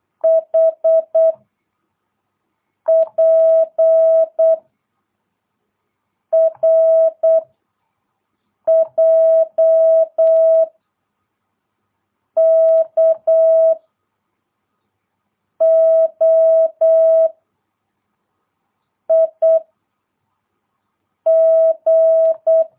Morse code, heard as a keyed tone, about 8 wpm, 650 Hz. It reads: H P R J K O I G
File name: jh8cbh / 2022cw